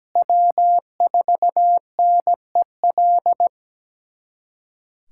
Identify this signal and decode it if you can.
Morse code, heard as a keyed tone, about 17 wpm, 700 Hz. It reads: W4NEL
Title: USA Callsign DE Callsign - 5 Characters - 17wpm